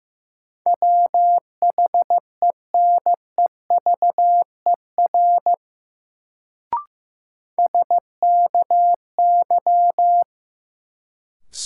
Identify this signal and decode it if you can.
Morse code, heard as a keyed tone, about 15 wpm, 700 Hz. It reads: WHENEVER SKY
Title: Sentences from Top 1200 Words - New Word Drill Alt - 15wpm